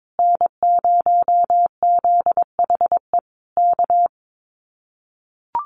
Morse code, heard as a keyed tone, about 22 wpm, 700 Hz. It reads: N0ZHE K
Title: Common QSO Phrases Encore 22wpm